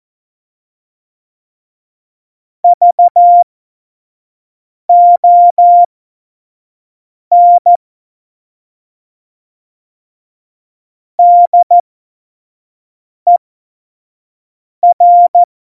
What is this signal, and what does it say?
Morse code, heard as a keyed tone, about 14 wpm, spaced as for 5 wpm, 700 Hz. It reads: VON DER
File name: Morsekurs / Witze / Witz 13 March 2018, 25 F70